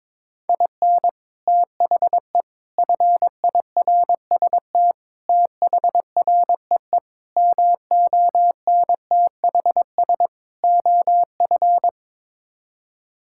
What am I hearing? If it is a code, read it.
Morse code, heard as a keyed tone, about 22 wpm, 700 Hz. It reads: IN THE FIRST THREE MONTHS OF
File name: practice / 22wpm / Sets of 6 Words - 22wpm